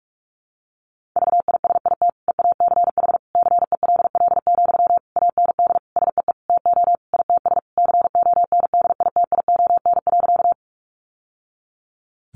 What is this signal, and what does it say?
Morse code, heard as a keyed tone, about 44 wpm, 700 Hz. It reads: VISIT EACH CELL, AND SEE TO ITS CONDITION.
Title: Sentences from Top 1100 Words 45wpm